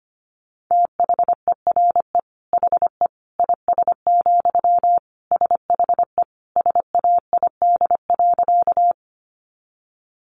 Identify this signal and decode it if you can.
Morse code, heard as a keyed tone, about 25 wpm, 700 Hz. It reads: THERE HE IS, SHE SAID.
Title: Sentences from Top 100 Words 25wpm